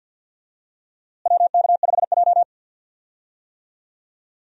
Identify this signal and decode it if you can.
Morse code, heard as a keyed tone, about 51 wpm, 700 Hz. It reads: WK5J
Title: USA Call Signs - 4 Characters - Repeated 2x - 50wpm